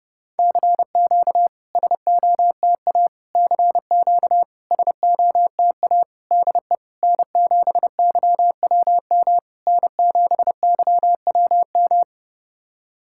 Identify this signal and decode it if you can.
Morse code, heard as a keyed tone, about 30 wpm, 700 Hz. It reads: CQ SOTA CQ SOTA DE N7YWM N7YWM
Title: SOTA QSO Practice 30wpm